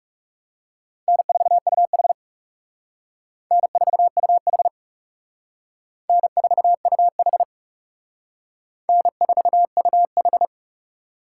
Morse code, heard as a keyed tone, about 36 wpm, 700 Hz. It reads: N4UH N4UH N4UH N4UH
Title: Speed Racing - USA 1x2 Call Signs 30wpm